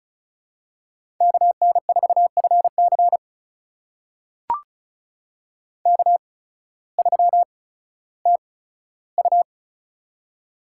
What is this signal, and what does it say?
Morse code, heard as a keyed tone, about 36 wpm, spaced as for 16 wpm, 700 Hz. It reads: KN4FC K3TU
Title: USA Call Signs - 8x Character Spacing - 35wpm